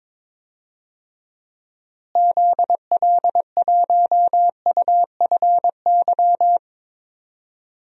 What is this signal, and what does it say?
Morse code, heard as a keyed tone, about 21 wpm, 700 Hz. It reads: ZL1UFY
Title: DXCC-100 Call Signs - 6 Characters - Repeated 2x - 22wpm